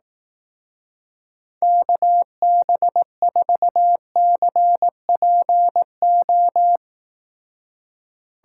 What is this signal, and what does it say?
Morse code, heard as a keyed tone, about 18 wpm, 700 Hz. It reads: KB4CPO
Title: Speed Racing - USA 2x3 Call Signs 15wpm